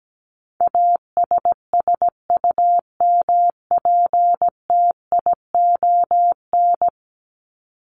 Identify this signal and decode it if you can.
Morse code, heard as a keyed tone, about 17 wpm, 700 Hz. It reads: ASSUMPTION
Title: Top 1751-2000 Words 17wpm